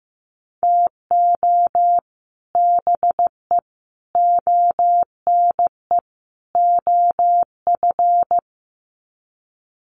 Morse code, heard as a keyed tone, about 15 wpm, 700 Hz. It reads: TO BE ONE OF
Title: Sets of 4 Words 15wpm